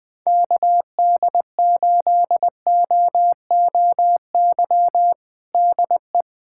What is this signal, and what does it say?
Morse code, heard as a keyed tone, about 20 wpm, 700 Hz. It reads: KD8OOY DE
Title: USA Callsign DE Callsign - 6 Characters - 20wpm